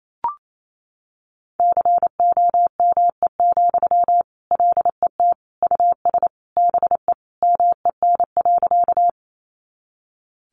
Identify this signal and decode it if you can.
Morse code, heard as a keyed tone, about 28 wpm, 700 Hz. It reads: COME, LET US BE MEN.
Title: Sentences from Top 300 Words 28wpm